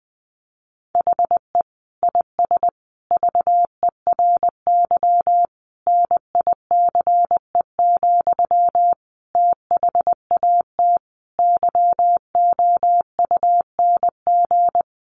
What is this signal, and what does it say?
Morse code, heard as a keyed tone, about 20 wpm, 700 Hz. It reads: HE IS VERY NICE, THAT YOUNG
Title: Sentences from Top 1100 Words 20wpm